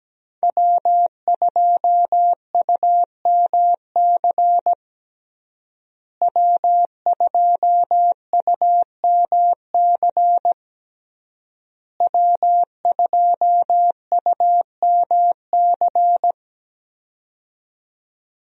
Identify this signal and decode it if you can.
Morse code, heard as a keyed tone, about 17 wpm, 700 Hz. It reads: W2UMC W2UMC W2UMC
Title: USA Call Signs - 5 Characters - Repeated 3x - 17wpm